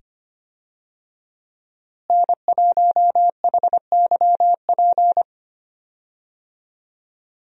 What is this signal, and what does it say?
Morse code, heard as a keyed tone, about 25 wpm, 690 Hz. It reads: N1HYP
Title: USA Call Signs - 5 Characters - Repeated 2x - 25wpm